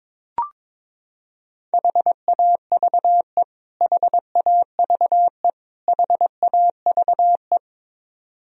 Morse code, heard as a keyed tone, about 22 wpm, 700 Hz. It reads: HAVE HAVE HAVE
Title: Top 100 Words Repeated 22wpm